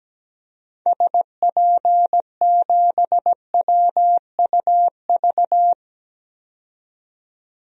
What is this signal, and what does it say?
Morse code, heard as a keyed tone, about 17 wpm, 700 Hz. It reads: SP7WUV
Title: DXCC-100 Call Signs - 6 Characters - 17wpm